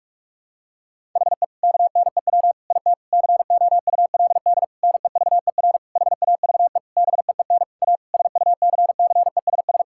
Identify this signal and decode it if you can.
Morse code, heard as a keyed tone, about 46 wpm, 700 Hz. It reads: HE KNEW IT COULD NEVER HAVE BEEN A SUCCESS
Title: Sentences from Top 1000 Words - Review Part 6 - 45wpm